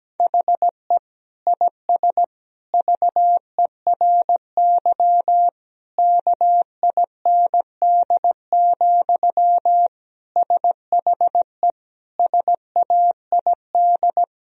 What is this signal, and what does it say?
Morse code, heard as a keyed tone, about 17 wpm, 700 Hz. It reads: HE IS VERY KIND, SHE SAID
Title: Sentences from Top 1000 Words - Review Part 1 - 17wpm